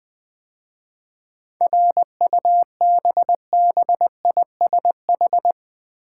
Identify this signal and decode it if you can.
Morse code, heard as a keyed tone, about 20 wpm, 700 Hz. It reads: RUBBISH
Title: 7 Letter Words - First 3 Letters Given - 20wpm